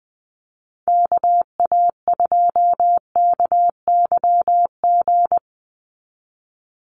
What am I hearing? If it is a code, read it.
Morse code, heard as a keyed tone, about 20 wpm, 700 Hz. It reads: KA2KYG
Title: USA Call Signs 20wpm